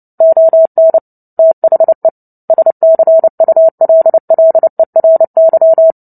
Morse code, heard as a keyed tone, about 30 wpm, 650 Hz. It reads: ON THE SCULLERY
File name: Book Two / 03-Chapter 3 - 29 WPM